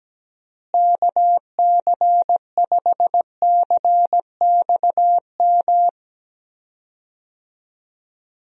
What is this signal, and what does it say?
Morse code, heard as a keyed tone, about 17 wpm, 700 Hz. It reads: KC5CXM
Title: USA Call Signs - 6 Characters - Repeated 3x - 17wpm